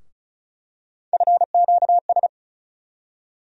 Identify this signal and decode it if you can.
Morse code, heard as a keyed tone, about 34 wpm, 700 Hz. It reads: FQS
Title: USA Call Sign Postfixes - 3 Letters 35wpm